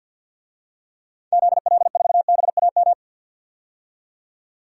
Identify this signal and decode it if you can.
Morse code, heard as a keyed tone, about 47 wpm, 700 Hz. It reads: ZL4BAM
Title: DXCC-100 Call Signs - 6 Characters - 50wpm